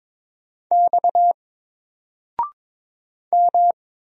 Morse code, heard as a keyed tone, about 22 wpm, 700 Hz. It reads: X M
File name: Single-Letter-Number-22wpm